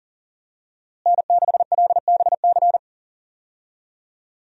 Morse code, heard as a keyed tone, about 40 wpm, 700 Hz. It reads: N6LBC
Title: USA Call Signs - 5 Characters - Repeated 3x - 40wpm